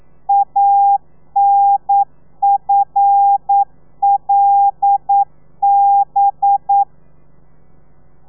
Morse code, heard as a keyed tone, about 9 wpm, 800 Hz. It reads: ANFLB
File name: telegrafia / i2tfj / Lez5 40